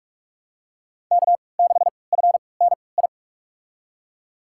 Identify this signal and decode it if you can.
Morse code, heard as a keyed tone, about 42 wpm, spaced as for 25 wpm, 700 Hz. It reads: K6FNI
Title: USA Call Signs - 3x Character Spacing - 45wpm